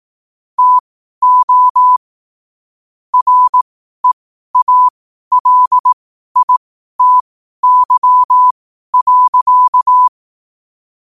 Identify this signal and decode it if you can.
Morse code, heard as a keyed tone, about 18 wpm, spaced as for 12 wpm, 1000 Hz. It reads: TO REALITY.
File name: qotdfiles / 12wpm